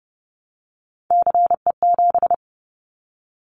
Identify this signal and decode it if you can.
Morse code, heard as a keyed tone, about 30 wpm, 700 Hz. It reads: CE7
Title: Callsign Prefixes for Top 100 DXCC - 30wpm